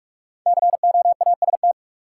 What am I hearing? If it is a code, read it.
Morse code, heard as a keyed tone, about 43 wpm, 700 Hz. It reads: COAST
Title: Sentences from Top 1000 Words - Review Part 4 - New Word Drill - 45wpm